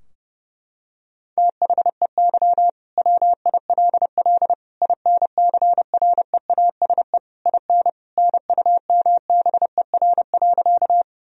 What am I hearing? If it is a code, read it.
Morse code, heard as a keyed tone, about 30 wpm, 700 Hz. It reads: THEY WILL INCREASE IN NUMBER.
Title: Sentences from Top 1000 Words - Review Part 4 - 30wpm